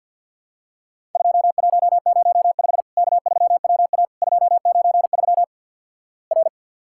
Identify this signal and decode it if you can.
Morse code, heard as a keyed tone, about 50 wpm, 700 Hz. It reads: 2105 K3WA 293 R
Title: Worked All Europe DX Contest - QTC Exchange - 50wpm